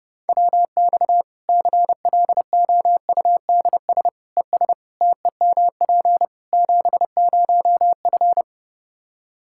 Morse code, heard as a keyed tone, about 30 wpm, 700 Hz. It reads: WX CLOUDS ES TEMP 70F
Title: Practice QSO - Exchange 2 - Info - 30wpm